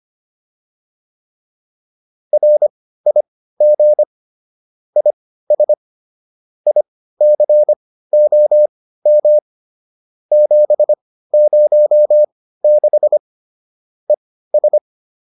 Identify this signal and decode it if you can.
Morse code, heard as a keyed tone, about 25 wpm, spaced as for 15 wpm, 600 Hz. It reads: RIG IS ICOM 706 ES
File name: QSO 302 15